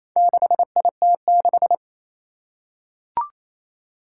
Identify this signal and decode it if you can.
Morse code, heard as a keyed tone, about 27 wpm, 700 Hz. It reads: 6IT6
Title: Four Challenging Letter-Number Combinations 28wpm